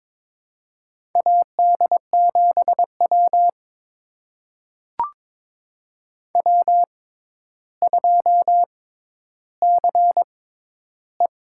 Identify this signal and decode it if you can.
Morse code, heard as a keyed tone, about 22 wpm, spaced as for 11 wpm, 700 Hz. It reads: AD7W W2CE
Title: USA Call Signs - 4 Characters - 6x Character Spacing - 22wpm